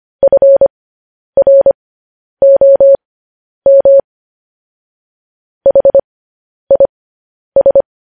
Morse code, heard as a keyed tone, about 25 wpm, spaced as for 10 wpm, 550 Hz. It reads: FROM HIS
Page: news-headlines-in-morse-code-at-10-wpm